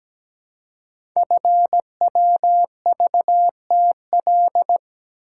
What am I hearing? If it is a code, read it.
Morse code, heard as a keyed tone, about 17 wpm, 700 Hz. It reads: FWVTL
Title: Code Groups - 5 Letter:Numbers - 17wpm